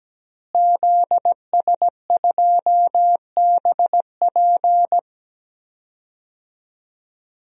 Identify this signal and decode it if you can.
Morse code, heard as a keyed tone, about 17 wpm, 700 Hz. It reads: ZS2BP
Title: DXCC-100 Call Signs - 5 Characters - Repeated 2x - 17wpm